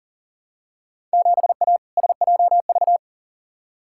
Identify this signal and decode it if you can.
Morse code, heard as a keyed tone, about 40 wpm, 700 Hz. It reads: 7A SJV
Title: Field Day Exchanges Encore - 40wpm